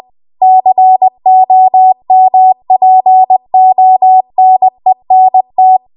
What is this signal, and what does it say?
Morse code, heard as a keyed tone, about 20 wpm, 750 Hz. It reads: COMPONENT